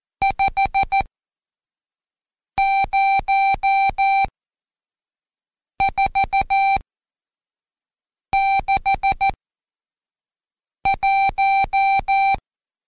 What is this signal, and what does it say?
Morse code, heard as a keyed tone, about 14 wpm, spaced as for 5 wpm, 780 Hz. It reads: 50461
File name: Code Quick / AudioDownld2 / Seg91-Day19-Review Numb